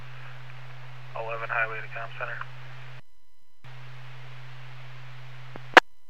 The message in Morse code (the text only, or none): none